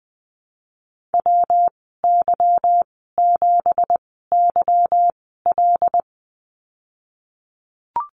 Morse code, heard as a keyed tone, about 20 wpm, spaced as for 15 wpm, 700 Hz. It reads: WY7YL
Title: 173 Lesson 28g - Number 9 - Call Signs 20wpm